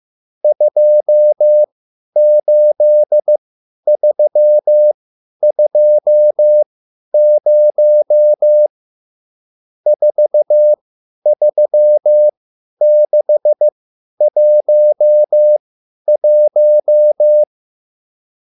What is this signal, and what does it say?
Morse code, heard as a keyed tone, about 15 wpm, spaced as for 10 wpm, 600 Hz. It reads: 28320 43611